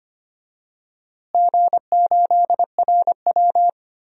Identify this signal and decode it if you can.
Morse code, heard as a keyed tone, about 25 wpm, 700 Hz. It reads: G8RW
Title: DXCC-100 Call Signs - 4 Characters - Repeated 2x - 25wpm